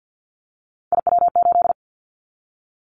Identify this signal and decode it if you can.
Morse code, heard as a keyed tone, about 50 wpm, 700 Hz. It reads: IW8
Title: Callsign Prefixes for Top 25 DXCC - 50wpm